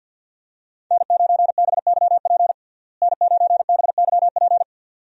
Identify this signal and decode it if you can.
Morse code, heard as a keyed tone, about 49 wpm, 700 Hz. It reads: N9BYP N9BYP
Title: USA Call Signs - 5 Characters - Repeated 2x - 50wpm